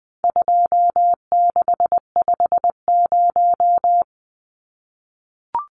Morse code, heard as a keyed tone, about 20 wpm, 700 Hz. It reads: 2650
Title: Numbers - 4-Digit 20wpm